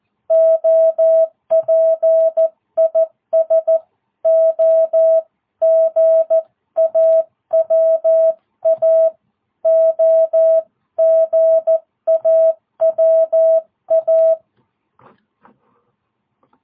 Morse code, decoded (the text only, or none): OP IS OGAWA OGAWA